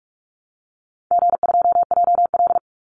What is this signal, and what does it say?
Morse code, heard as a keyed tone, about 44 wpm, 700 Hz. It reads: G2JL
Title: Easy Call Signs Repeated 3x - 45wpm